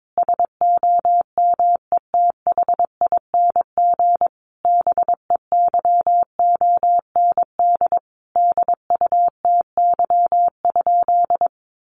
Morse code, heard as a keyed tone, about 22 wpm, 700 Hz. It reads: SOMETHING BEYOND DUTY?
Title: Sentences from Top 1500 Words 22wpm